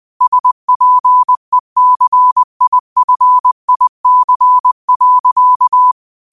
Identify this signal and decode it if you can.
Morse code, heard as a keyed tone, about 20 wpm, 1000 Hz. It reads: SPECIFIC.